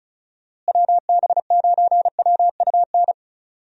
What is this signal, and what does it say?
Morse code, heard as a keyed tone, about 36 wpm, 700 Hz. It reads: WB9WUN
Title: USA Call Signs - 6 Characters - Repeated 2x - 35wpm